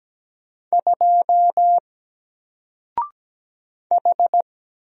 Morse code, heard as a keyed tone, about 17 wpm, 700 Hz. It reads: 2 H
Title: Single Character (Letter:Number:Punctuation:Prosign) 17wpm